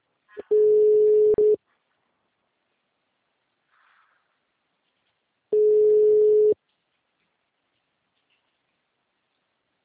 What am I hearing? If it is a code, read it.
Ringback tone (Europe, 425 Hz).